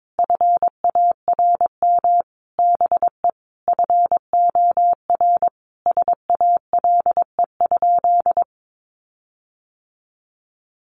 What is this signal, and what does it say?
Morse code, heard as a keyed tone, about 22 wpm, 700 Hz. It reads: FARM BE FOR SALE?
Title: Sentences from Top 1400 Words 22wpm